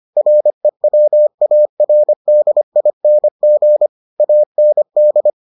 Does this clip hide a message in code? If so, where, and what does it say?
Yes: Morse code from 0.2 to 5.3 s, heard as a keyed tone, about 25 wpm, 600 Hz. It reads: REWARDING AND